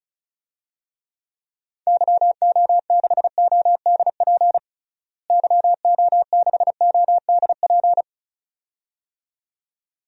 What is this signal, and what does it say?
Morse code, heard as a keyed tone, about 36 wpm, 700 Hz. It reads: YO6ODP YO6ODP
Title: DXCC-100 Call Signs - 6 Characters - Repeated 2x - 35wpm